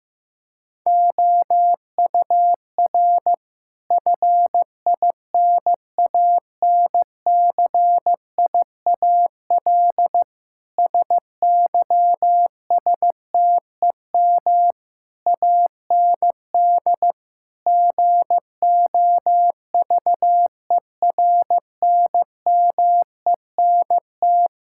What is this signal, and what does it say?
Morse code, heard as a keyed tone, about 16 wpm, 700 Hz. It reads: OUR FINANCIAL SYSTEM AND GOVERNMENT
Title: Sentences from Top 1000 Words - Review Part 5 - 15wpm